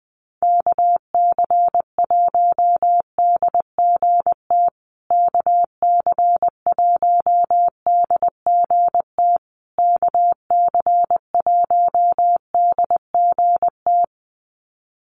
Morse code, heard as a keyed tone, about 20 wpm, 700 Hz. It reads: KC1DGT KC1DGT KC1DGT
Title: USA Call Signs Repeated 3x 20wpm